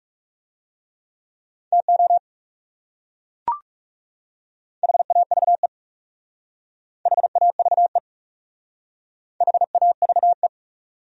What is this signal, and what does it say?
Morse code, heard as a keyed tone, about 36 wpm, 700 Hz. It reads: TO HAVE HAVE HAVE
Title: Speed-Racing Top 100 Words 30wpm